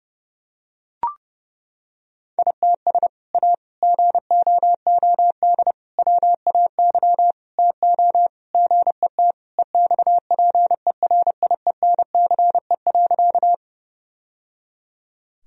Morse code, heard as a keyed tone, about 30 wpm, 700 Hz. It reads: ITS A GOOD WAY TO GET EXPERIENCE.
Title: Sentences from Top 1000 Words - Review Part 3 - 30wpm